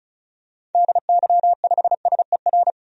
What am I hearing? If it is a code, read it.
Morse code, heard as a keyed tone, about 34 wpm, 700 Hz. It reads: DY5SER
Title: DXCC-100 Call Signs - 6 Characters - 35wpm